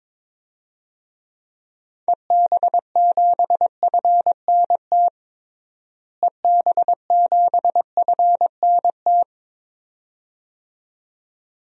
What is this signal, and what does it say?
Morse code, heard as a keyed tone, about 21 wpm, 700 Hz. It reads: EB7FNT EB7FNT
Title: DXCC-100 Call Signs - 6 Characters - Repeated 2x - 22wpm